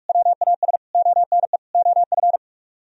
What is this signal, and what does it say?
Morse code, heard as a keyed tone, about 56 wpm, 700 Hz. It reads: WAS ONE OF
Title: Speed-Racing - Sets of 4 Words 45wpm